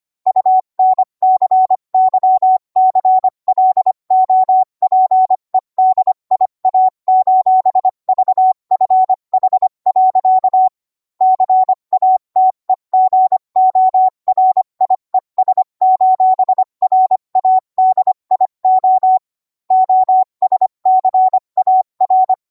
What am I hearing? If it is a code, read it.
Morse code, heard as a keyed tone, about 25 wpm, 750 Hz. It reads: UNCYCLOPEDIA:VFH. CATEGORIES:RADIO OSCAR